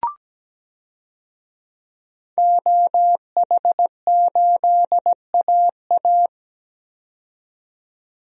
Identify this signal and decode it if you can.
Morse code, heard as a keyed tone, about 17 wpm, 700 Hz. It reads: OH8AA
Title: DXCC-100 Call Signs - 5 Characters - 17wpm